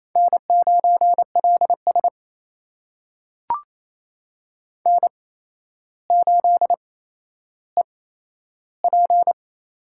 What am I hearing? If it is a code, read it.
Morse code, heard as a keyed tone, about 28 wpm, spaced as for 12 wpm, 700 Hz. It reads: N9LS N8EP
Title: USA Call Signs - 8x Character Spacing - 28wpm